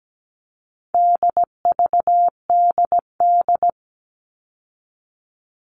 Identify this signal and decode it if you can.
Morse code, heard as a keyed tone, about 17 wpm, 700 Hz. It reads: DVDD
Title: All Cut Numbers - 4-Digit 17wpm